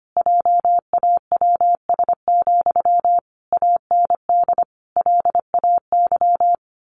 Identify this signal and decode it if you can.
Morse code, heard as a keyed tone, about 25 wpm, 700 Hz. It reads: JAWS, AND LAY
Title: A Sound Of Thunder 25wpm